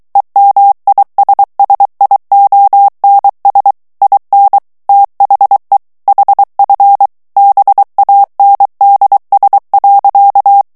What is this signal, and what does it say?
Morse code, heard as a keyed tone, about 23 wpm, 800 Hz. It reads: EMISSIONS IN THE HF BANDS.
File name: ANGLAIS texte clair 3 vitesse 22 mots minute